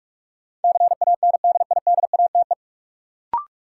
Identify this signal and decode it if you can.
Morse code, heard as a keyed tone, about 46 wpm, 700 Hz. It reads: CANDIDATE
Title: Sentences from Top 1600 Words - New Word Drill Alt - 45wpm